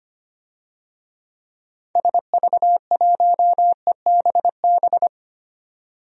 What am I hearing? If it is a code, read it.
Morse code, heard as a keyed tone, about 25 wpm, 700 Hz. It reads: SV1EBB
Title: DXCC-100 Call Signs - 6 Characters - 25wpm